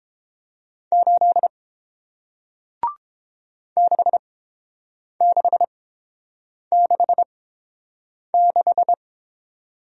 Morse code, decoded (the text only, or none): 8 6666